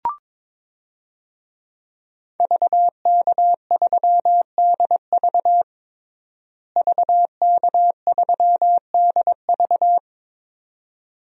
Morse code, heard as a keyed tone, about 22 wpm, 700 Hz. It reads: VK3DV VK3DV
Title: DXCC-100 Call Signs - 5 Characters - Repeated 2x - 22wpm